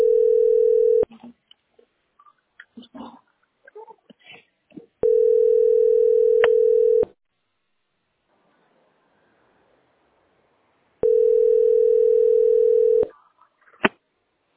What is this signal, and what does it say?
Ringback tone (North America), 440 + 480 Hz.